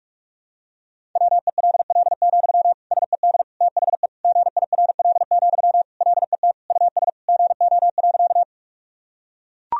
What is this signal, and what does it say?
Morse code, heard as a keyed tone, about 42 wpm, 700 Hz. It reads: WELL, SED THE GIRL, LET US GO.
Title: Sentences from Top 400 Words and Abbreviations 45wpm